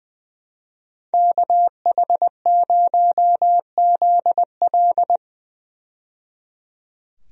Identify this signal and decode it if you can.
Morse code, heard as a keyed tone, about 20 wpm, 700 Hz. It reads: KH0ZL
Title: USA Call Signs - 5 Characters - 20wpm